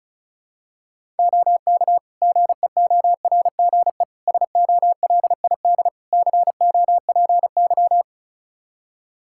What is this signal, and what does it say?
Morse code, heard as a keyed tone, about 34 wpm, 700 Hz. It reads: OK GEORGE SOLID COPY
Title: K1USN SST QSO Practice 35wpm